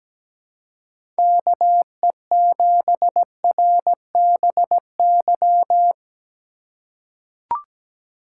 Morse code, heard as a keyed tone, about 17 wpm, 700 Hz. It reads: KE7RBY
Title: USA Call Signs - 6 Characters - 4x Character Spacing - 17wpm